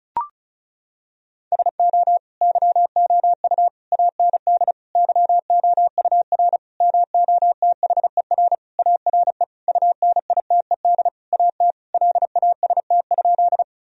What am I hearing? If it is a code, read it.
Morse code, heard as a keyed tone, about 36 wpm, 700 Hz. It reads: SO YOU AND YOUR MOTHER ARE UNITED AT LAST?
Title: Sentences from Top 1000 Words - Review Part 4 - 35wpm